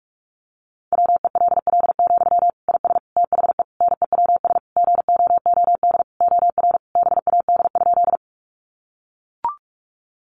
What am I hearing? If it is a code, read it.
Morse code, heard as a keyed tone, about 44 wpm, 700 Hz. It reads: WELL, IS THE NEWS GOOD OR BAD?
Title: Sentences from Top 800 Words 45wpm